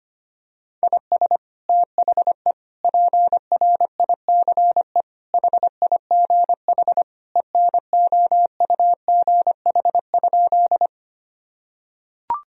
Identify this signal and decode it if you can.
Morse code, heard as a keyed tone, about 25 wpm, 700 Hz. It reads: IS THE PRICE HIGH ENOUGH?
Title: Sentences from Top 1000 Words - Review Part 4 - 25wpm